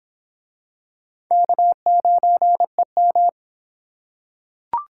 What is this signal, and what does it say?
Morse code, heard as a keyed tone, about 26 wpm, 700 Hz. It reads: K9EM